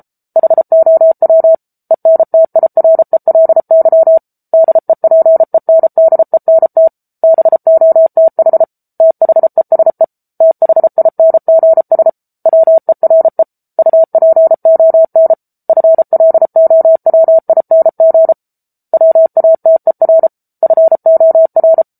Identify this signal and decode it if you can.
Morse code, heard as a keyed tone, about 34 wpm, 650 Hz. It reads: HOW ENTIRELY DEPENDENT BOTH THESE THINGS WERE UPON FLOWING WATER FOR
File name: Book Two / 07-Chapter 7 - 33 WPM